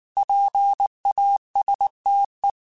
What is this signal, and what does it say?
Morse code, heard as a keyed tone, about 19 wpm, 800 Hz. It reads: PASTE
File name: Alphabetical / Alphabetical mp3s / Ps @18